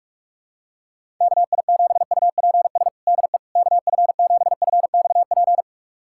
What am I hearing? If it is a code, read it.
Morse code, heard as a keyed tone, about 46 wpm, 700 Hz. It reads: KI7UPS DE KF7FXP
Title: USA Callsign DE Callsign - 6 Characters - 45wpm